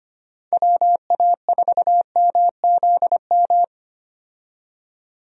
Morse code, heard as a keyed tone, about 25 wpm, 700 Hz. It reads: WA4MZM
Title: USA Call Signs - 6 Characters - 25wpm